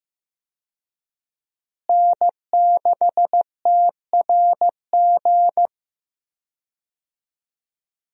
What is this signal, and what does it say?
Morse code, heard as a keyed tone, about 15 wpm, 700 Hz. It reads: N6TRG
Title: USA Call Signs - 5 Characters - Repeated 3x - 15wpm